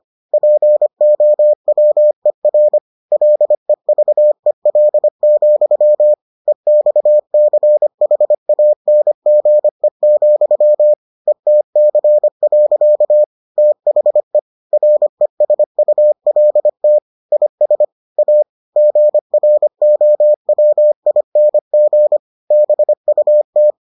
Morse code, heard as a keyed tone, about 25 wpm, 600 Hz. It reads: POWER LEVEL, EXCHANGE, ETC. THE RESULT IS A GROWING BUT